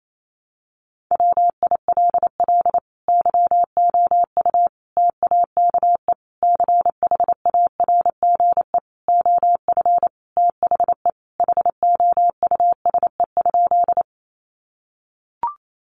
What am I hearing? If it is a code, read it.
Morse code, heard as a keyed tone, about 28 wpm, 700 Hz. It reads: WILL YOU TAKE CHARGE OF THE HOUSE?
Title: Sentences from Top 900 Words 28wpm